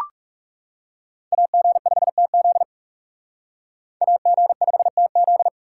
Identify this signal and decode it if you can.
Morse code, heard as a keyed tone, about 39 wpm, 700 Hz. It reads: AG5TZ AG5TZ